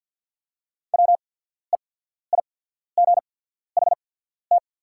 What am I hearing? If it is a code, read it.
Morse code, heard as a keyed tone, about 48 wpm, spaced as for 14 wpm, 690 Hz. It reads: WEIGHT
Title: 8x Character Spacing - Top 500 Words - 50wpm